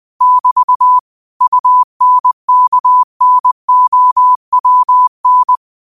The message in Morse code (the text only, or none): <BT> UNKNOWN